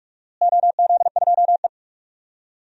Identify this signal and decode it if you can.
Morse code, heard as a keyed tone, about 47 wpm, 700 Hz. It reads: OZ2E
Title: DXCC-100 Call Signs - 4 Characters - Repeated 2x - 45wpm